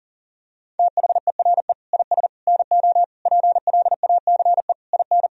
Morse code, heard as a keyed tone, about 40 wpm, 700 Hz. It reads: THERE IS NO PLACE IN
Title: Sentences from Top 1000 Words - Review Part 4 - 40wpm